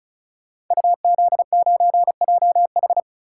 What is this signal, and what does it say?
Morse code, heard as a keyed tone, about 36 wpm, 700 Hz. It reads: UZ9JH
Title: Code Groups - 5 Letter:Numbers - 35wpm